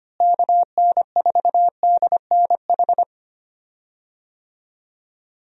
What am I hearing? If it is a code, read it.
Morse code, heard as a keyed tone, about 25 wpm, 700 Hz. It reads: KN4DNH
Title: USA Call Signs - 6 Characters - Repeated 3x - 25wpm